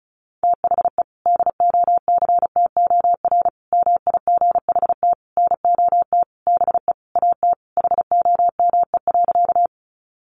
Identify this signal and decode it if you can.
Morse code, heard as a keyed tone, about 34 wpm, 700 Hz. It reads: THE DOCTOR MIGHT NOT BE AT HOME.
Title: Sentences from Top 1500 Words 35wpm